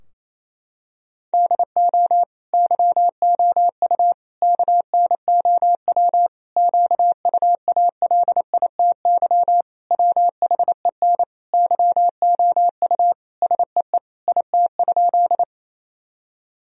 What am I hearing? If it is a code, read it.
Morse code, heard as a keyed tone, about 28 wpm, 700 Hz. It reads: DO YOU KNOW QUALITY WHEN YOU SEE IT?
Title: Sentences from Top 1000 Words - Review Part 4 - 28wpm